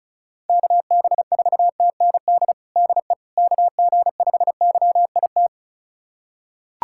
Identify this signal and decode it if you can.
Morse code, heard as a keyed tone, about 36 wpm, 700 Hz. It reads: KB4TND DE KG5YIT
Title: USA Callsign DE Callsign - 6 Characters - 35wpm